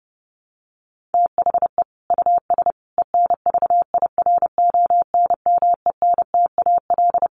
Morse code, heard as a keyed tone, about 30 wpm, 700 Hz. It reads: THE US ENVIRONMENTAL
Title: Sets of 5 Words Encore2 30wpm